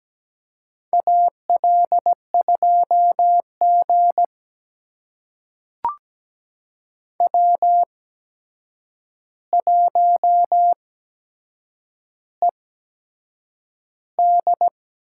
Morse code, heard as a keyed tone, about 17 wpm, spaced as for 7 wpm, 700 Hz. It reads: AL2G W1ED
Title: USA Call Signs - 4 Characters - 8x Character Spacing - 17wpm